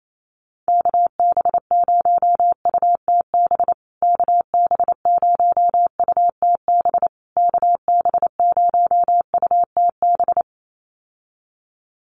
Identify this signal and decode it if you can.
Morse code, heard as a keyed tone, about 28 wpm, 690 Hz. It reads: KB0UTB KB0UTB KB0UTB
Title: USA Call Signs Repeated 3x 28wpm